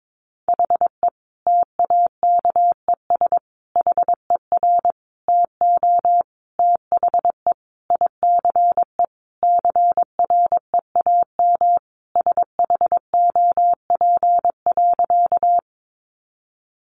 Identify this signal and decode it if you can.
Morse code, heard as a keyed tone, about 22 wpm, 700 Hz. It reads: HE TAKES HER TO THE ICE CREAM SHOP.
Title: Additional Easy Phrases 22wpm